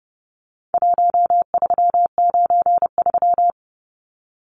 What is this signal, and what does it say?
Morse code, heard as a keyed tone, about 30 wpm, 700 Hz. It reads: 1393
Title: Numbers - 4-Digit 30wpm